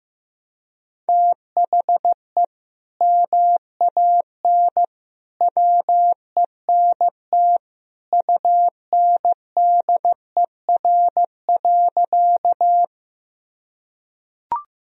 Morse code, heard as a keyed tone, about 15 wpm, 700 Hz. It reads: THE MAN WENT UNDER.
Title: Sentences from Top 1000 Words - Review Part 1 - 15wpm